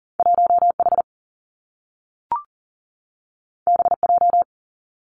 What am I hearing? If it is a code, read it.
Morse code, heard as a keyed tone, about 40 wpm, 700 Hz. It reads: 1 H B J
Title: Two Challenging Letter-Number Combinations 40wpm